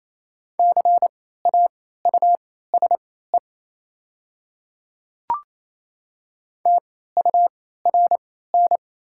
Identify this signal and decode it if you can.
Morse code, heard as a keyed tone, about 28 wpm, spaced as for 16 wpm, 700 Hz. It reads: CAUSE TURN
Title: 3x Character Spacing - Top 300 Words - 28wpm